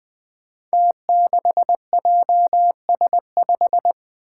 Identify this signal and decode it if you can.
Morse code, heard as a keyed tone, about 20 wpm, 700 Hz. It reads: T6JS5
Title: Five Challenging Letter-Number Combinations 20wpm